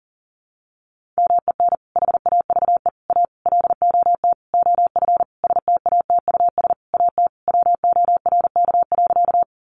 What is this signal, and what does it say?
Morse code, heard as a keyed tone, about 40 wpm, 700 Hz. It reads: MEN HAVE A LOT OF STATUS AT WORK.
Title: Sentences from Top 1400 Words 40wpm